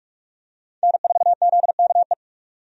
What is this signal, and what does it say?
Morse code, heard as a keyed tone, about 47 wpm, 700 Hz. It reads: N4ZKE
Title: USA Call Signs - 5 Characters - 45wpm